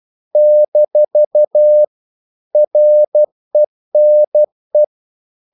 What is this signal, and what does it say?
Morse code, heard as a keyed tone, about 12 wpm, 600 Hz. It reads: - RENE
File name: CW / Practice / EnglishQuotations0112